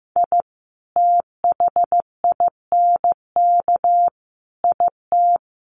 Morse code, heard as a keyed tone, about 15 wpm, 700 Hz. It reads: I THINK IT
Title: Sets of 5 Words Encore2 15wpm